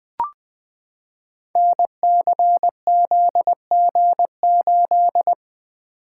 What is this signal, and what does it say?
Morse code, heard as a keyed tone, about 20 wpm, 700 Hz. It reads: NCZG8
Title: Code Groups - 5 Letter:Numbers - 20wpm